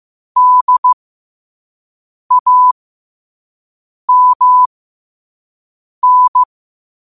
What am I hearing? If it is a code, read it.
Morse code, heard as a keyed tone, about 15 wpm, spaced as for 5 wpm, 1000 Hz. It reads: DAMN
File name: qotdfiles / 5wpm